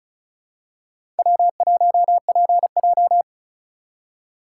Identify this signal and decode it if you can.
Morse code, heard as a keyed tone, about 34 wpm, 700 Hz. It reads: W1PJ
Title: USA Call Signs - 4 Characters - 6x Character Spacing - 35wpm